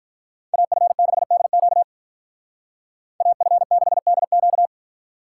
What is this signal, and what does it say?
Morse code, heard as a keyed tone, about 44 wpm, 700 Hz. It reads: AF6DQ AF6DQ
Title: Speed Racing - USA 2x2 Call Signs 35wpm